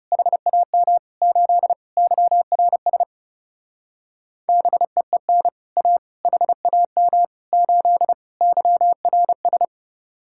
Morse code, heard as a keyed tone, about 38 wpm, 700 Hz. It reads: HAM 8 YRS BEEN A HAM 8 YRS